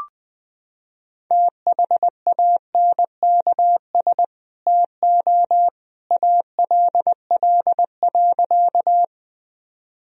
Morse code, heard as a keyed tone, about 19 wpm, 700 Hz. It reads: THANKS TO ALL.
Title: Sentences from Top 1000 Words - Review Part 3 - 20wpm